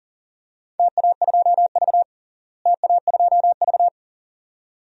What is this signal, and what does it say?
Morse code, heard as a keyed tone, about 40 wpm, 700 Hz. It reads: TA2V TA2V
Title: DXCC-100 Call Signs - 4 Characters - Repeated 2x - 40wpm